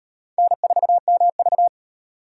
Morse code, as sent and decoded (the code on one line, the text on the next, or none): -. ....- -- ...-
N4MV